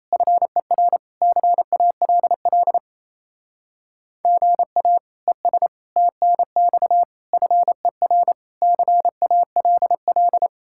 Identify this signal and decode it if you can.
Morse code, heard as a keyed tone, about 35 wpm, 700 Hz. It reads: FER CALL GA ES TNX FER CALL
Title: Speed-Racing Common QSO Phrases - 28wpm